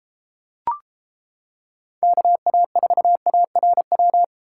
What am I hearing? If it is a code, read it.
Morse code, heard as a keyed tone, about 29 wpm, 700 Hz. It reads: KA4ARW